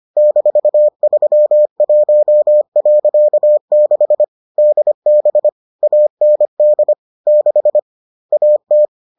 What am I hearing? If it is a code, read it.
Morse code, heard as a keyed tone, about 25 wpm, 600 Hz. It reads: -31.6 DB AND 6 AT